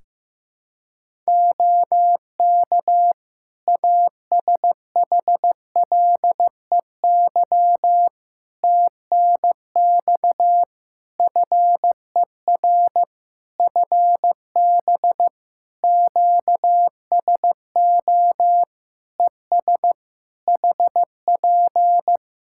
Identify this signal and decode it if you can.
Morse code, heard as a keyed tone, about 15 wpm, 700 Hz. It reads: OK ASHLEY TNX FER FB QSO ES HP
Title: Practice QSO - Exchange 4 - Ending - 15wpm